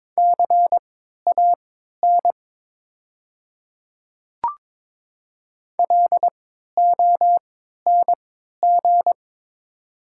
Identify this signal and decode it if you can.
Morse code, heard as a keyed tone, about 22 wpm, spaced as for 12 wpm, 700 Hz. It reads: CAN LONG
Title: 3x Character Spacing - Top 200 Words - 22wpm